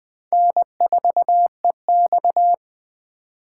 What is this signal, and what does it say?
Morse code, heard as a keyed tone, about 20 wpm, 700 Hz. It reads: N4EX
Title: 2023 POTA Hunter Call Signs - 20wpm